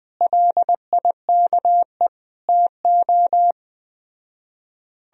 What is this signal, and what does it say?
Morse code, heard as a keyed tone, about 21 wpm, 700 Hz. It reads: LIKE TO